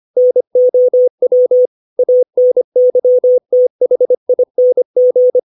Morse code, heard as a keyed tone, about 25 wpm, 500 Hz. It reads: NOW ANYTHING
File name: 632 words / 632 Words 25WPM